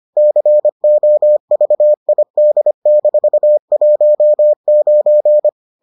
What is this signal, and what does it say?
Morse code, heard as a keyed tone, about 25 wpm, 600 Hz. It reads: COVID-19